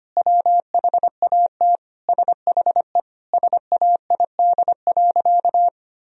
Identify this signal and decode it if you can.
Morse code, heard as a keyed tone, about 25 wpm, 700 Hz. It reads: WHAT SHE SAID.
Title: Sentences from Top 1000 Words - Review Part 4 - 25wpm